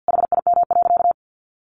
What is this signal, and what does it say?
Morse code, heard as a keyed tone, about 50 wpm, 700 Hz. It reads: HIM.
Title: Sentences from Top 1400 Words 50wpm